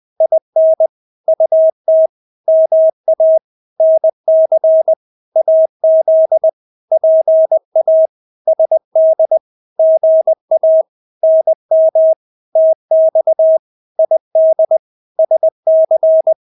Morse code, heard as a keyed tone, about 20 wpm, 640 Hz. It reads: IN UT MA NC AZ PA SD GA NM TX ID SC
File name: State-Abbreviations-20